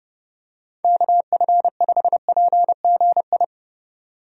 Morse code, heard as a keyed tone, about 30 wpm, 700 Hz. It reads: KF5PGI
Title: USA Call Signs - 6 Characters - 30wpm